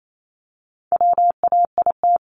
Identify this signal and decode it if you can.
Morse code, heard as a keyed tone, about 28 wpm, 700 Hz. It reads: WAIT
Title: ING Suffix - Top 100 Words with Base Words 28wpm